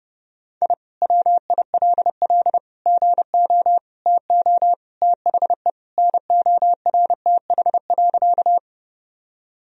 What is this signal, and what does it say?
Morse code, heard as a keyed tone, about 28 wpm, 700 Hz. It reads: I WILL GO TO THE NORTH.
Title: Sentences from Top 1000 Words - Review Part 1 - 30wpm